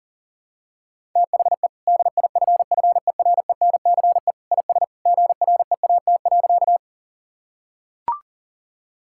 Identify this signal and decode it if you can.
Morse code, heard as a keyed tone, about 37 wpm, 700 Hz. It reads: THE DIFFERENCE IS GREAT.
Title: Sentences from Top 1000 Words - Review Part 5 - 40wpm